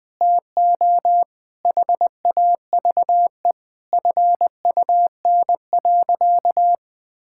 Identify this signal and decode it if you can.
Morse code, heard as a keyed tone, about 20 wpm, 700 Hz. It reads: TO HAVE FUN.
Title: Sentences with 4-Word Phrases - 20wpm